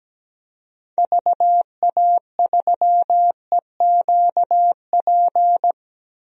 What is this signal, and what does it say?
Morse code, heard as a keyed tone, about 17 wpm, 700 Hz. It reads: VA3EQP
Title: DXCC-100 Call Signs - 6 Characters - Repeated 3x - 17wpm